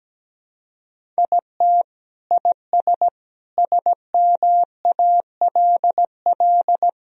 Morse code, heard as a keyed tone, about 17 wpm, 700 Hz. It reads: IT IS SMALL
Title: Sentences from Top 1000 Words - Review Part 1 - 17wpm